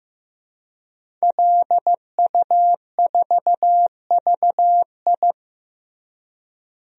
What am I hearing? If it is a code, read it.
Morse code, heard as a keyed tone, about 16 wpm, 700 Hz. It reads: LU4VI
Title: DXCC-100 Call Signs - 5 Characters - Repeated 2x - 15wpm